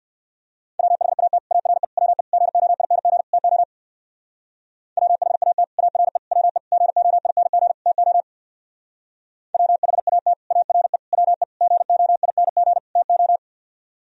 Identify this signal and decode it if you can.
Morse code, heard as a keyed tone, about 59 wpm, 700 Hz. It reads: WHAT ARE WE GOING TO WHAT ARE WE GOING TO WHAT ARE WE GOING TO
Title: Speed-Racing - Sets of 5 Words 50wpm